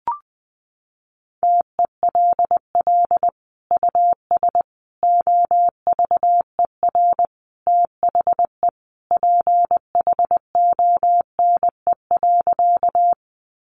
Morse code, heard as a keyed tone, about 20 wpm, 700 Hz. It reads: TELL US OVER THE PHONE.
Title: Sentences from Top 1100 Words 20wpm